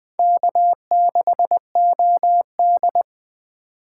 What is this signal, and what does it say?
Morse code, heard as a keyed tone, about 20 wpm, 700 Hz. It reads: K6OD